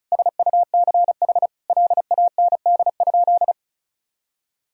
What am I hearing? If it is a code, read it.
Morse code, heard as a keyed tone, about 34 wpm, 700 Hz. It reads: SUCH LAND?
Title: Sentences from Top 1000 Words - Review Part 1 - 35wpm